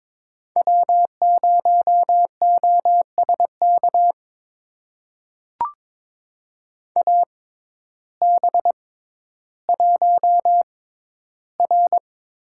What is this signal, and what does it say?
Morse code, heard as a keyed tone, about 22 wpm, spaced as for 12 wpm, 700 Hz. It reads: W0OSK AB1R
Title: USA Call Signs - 5 Characters - 6x Character Spacing - 22wpm